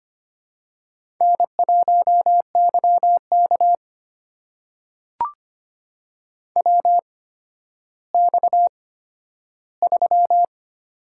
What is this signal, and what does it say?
Morse code, heard as a keyed tone, about 25 wpm, spaced as for 12 wpm, 700 Hz. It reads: N1YK WX3
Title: USA Call Signs - 4 Characters - 8x Character Spacing - 25wpm